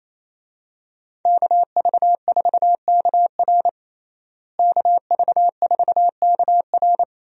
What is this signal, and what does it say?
Morse code, heard as a keyed tone, about 27 wpm, 700 Hz. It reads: KV4KR KV4KR
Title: USA Call Signs - 5 Characters - Repeated 3x - 28wpm